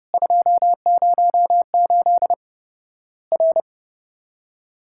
Morse code, heard as a keyed tone, about 30 wpm, 700 Hz. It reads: 208 R
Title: Worked All Europe DX Contest - QTC Exchange - 30wpm